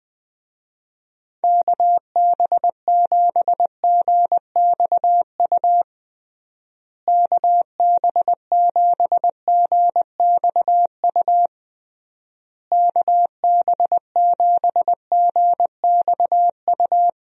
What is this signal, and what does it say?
Morse code, heard as a keyed tone, about 20 wpm, 700 Hz. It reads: KB7GXU KB7GXU KB7GXU
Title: USA Call Signs - 6 Characters - Repeated 3x - 20wpm